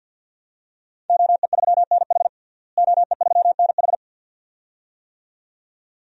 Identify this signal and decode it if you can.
Morse code, heard as a keyed tone, about 50 wpm, 700 Hz. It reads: OE3NH OE3NH
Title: DXCC-100 Call Signs - 5 Characters - Repeated 2x - 50wpm